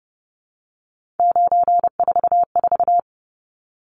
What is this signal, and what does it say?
Morse code, heard as a keyed tone, about 30 wpm, 700 Hz. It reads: 944
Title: Numbers - 3-Digit 30wpm